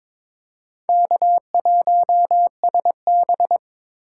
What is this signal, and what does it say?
Morse code, heard as a keyed tone, about 26 wpm, 700 Hz. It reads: K1SB
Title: Speed Racing - USA 1x2 Call Signs 22wpm